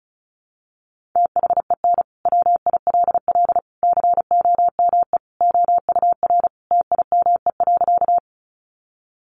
Morse code, heard as a keyed tone, about 34 wpm, 700 Hz. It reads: THEN WILL COME OUR TIME.
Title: Sentences from Top 200 Words 35wpm